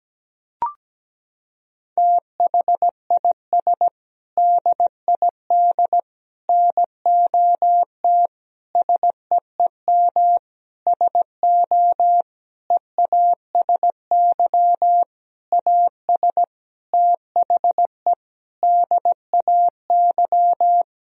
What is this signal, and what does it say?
Morse code, heard as a keyed tone, about 17 wpm, 700 Hz. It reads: THIS DID NOT SEEM SO EASY AS THE DAY
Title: Sentences from Top 1000 Words - Review Part 1 - 17wpm